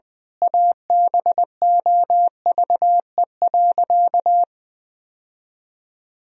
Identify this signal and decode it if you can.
Morse code, heard as a keyed tone, about 20 wpm, 700 Hz. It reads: ABOVE.
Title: Sentences from Top 1000 Words - Review Part 2 - 20wpm